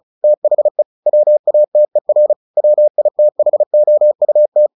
Morse code, heard as a keyed tone, about 34 wpm, 600 Hz. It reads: THE WATER WITHOUT
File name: the old man and the sea 2 35wpm